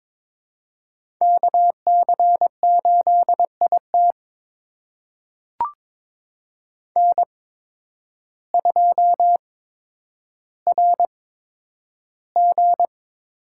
Morse code, heard as a keyed tone, about 23 wpm, spaced as for 10 wpm, 700 Hz. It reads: KC8IT N2RG
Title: USA Call Signs - 5 Characters - 8x Character Spacing - 22wpm